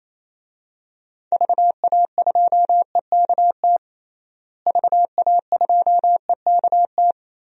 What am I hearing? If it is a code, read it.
Morse code, heard as a keyed tone, about 28 wpm, 700 Hz. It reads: VA2EKT VA2EKT
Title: DXCC-100 Call Signs - 6 Characters - Repeated 2x - 28wpm